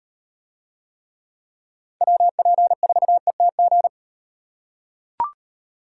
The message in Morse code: WP4ETG

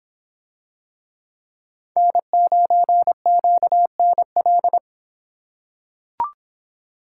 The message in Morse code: N9QNL

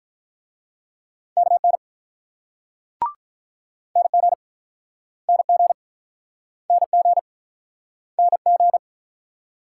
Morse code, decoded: KN NG NG NG NG

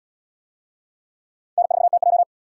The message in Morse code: N4TJ

